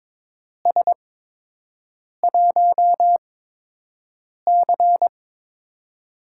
Morse code, heard as a keyed tone, about 21 wpm, spaced as for 6 wpm, 700 Hz. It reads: S1C
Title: USA Call Signs - 8x Character Spacing - 22wpm